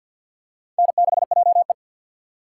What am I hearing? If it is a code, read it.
Morse code, heard as a keyed tone, about 50 wpm, 700 Hz. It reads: N6JE
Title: USA Call Signs - 2x Character Spacing - 50wpm